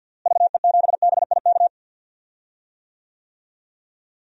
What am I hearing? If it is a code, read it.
Morse code, heard as a keyed tone, about 47 wpm, 700 Hz. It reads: VE7BIK
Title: DXCC-100 Call Signs - 6 Characters - Repeated 3x - 50wpm